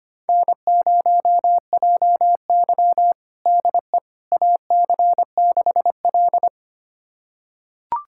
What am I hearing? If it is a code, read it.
Morse code, heard as a keyed tone, about 25 wpm, 700 Hz. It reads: N0JY DE AC6L
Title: USA Callsign DE Callsign - 4 Characters - 25wpm